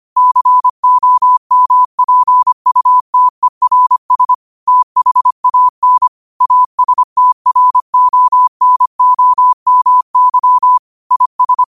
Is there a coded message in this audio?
Yes: Morse code from 0.2 to 11.6 s, heard as a keyed tone, about 25 wpm, 1000 Hz. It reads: COMPUTERS THAN ASTRONOMY IS